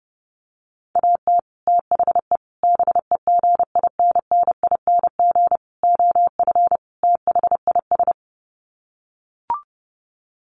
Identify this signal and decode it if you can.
Morse code, heard as a keyed tone, about 30 wpm, 700 Hz. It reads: AT THE BEGINNING OF THIS
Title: Sets of 5 Words Encore 30wpm